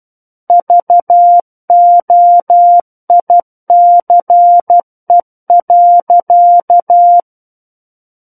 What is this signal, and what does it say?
Morse code, heard as a keyed tone, about 12 wpm, 700 Hz. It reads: VOICE.
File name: Book One / 03-Chapter 3 - 12 WPM